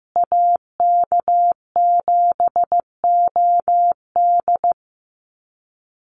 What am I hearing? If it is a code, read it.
Morse code, heard as a keyed tone, about 15 wpm, 700 Hz. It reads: AK7OD
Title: USA Call Signs Repeated 3x 15wpm